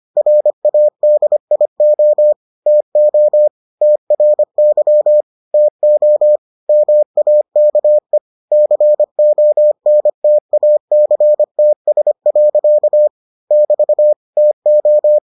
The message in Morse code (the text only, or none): RADIO TO TRY TO MAKE CONTACTS. <BT> TO